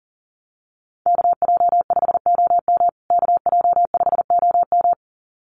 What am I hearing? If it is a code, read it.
Morse code, heard as a keyed tone, about 40 wpm, 700 Hz. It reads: KJ5OM KJ5OM
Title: USA Call Signs Repeated 2x 40wpm